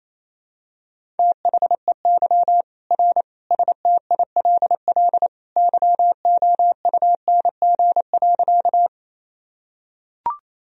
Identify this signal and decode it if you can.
Morse code, heard as a keyed tone, about 28 wpm, 700 Hz. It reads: THEY R STILL YOUNG.
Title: Sentences from Top 400 Words and Abbreviations 28wpm